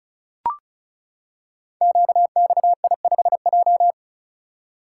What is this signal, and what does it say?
Morse code, heard as a keyed tone, about 36 wpm, 700 Hz. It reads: QXI5J
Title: Code Groups - 5 Letter:Numbers - 35wpm